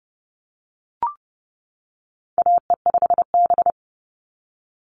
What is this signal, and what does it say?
Morse code, heard as a keyed tone, about 30 wpm, 700 Hz. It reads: AE5B
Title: NA SOTA Chaser Callsigns 30wpm